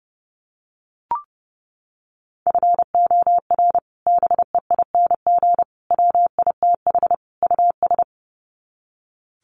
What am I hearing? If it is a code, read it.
Morse code, heard as a keyed tone, about 30 wpm, 700 Hz. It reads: FOR BEING WITH US
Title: Sets of 4 Words 30wpm